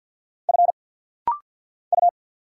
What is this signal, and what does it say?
Morse code, heard as a keyed tone, about 48 wpm, 700 Hz. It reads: F U
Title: Single Letter-Number - Rapid-Fire 50wpm